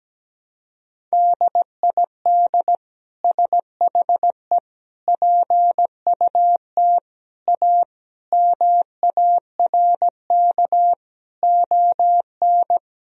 Morse code, heard as a keyed tone, about 17 wpm, 700 Hz. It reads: DID SHE PUT A MARK ON
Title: Sentences from Top 1000 Words - Review Part 1 - 17wpm